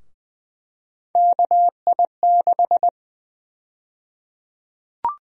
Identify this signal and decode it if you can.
Morse code, heard as a keyed tone, about 20 wpm, 700 Hz. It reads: KI6
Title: USA Call Sign Prefixes - Repeated 2x - 20wpm